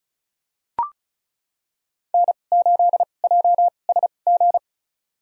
Morse code, heard as a keyed tone, about 36 wpm, spaced as for 26 wpm, 700 Hz. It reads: N8JSG
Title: USA Call Signs - 5 Characters - 2x Character Spacing - 35wpm